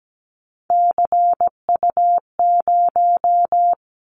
Morse code, heard as a keyed tone, about 17 wpm, 700 Hz. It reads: CU0
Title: Callsign Prefixes for Top 100 DXCC - 17wpm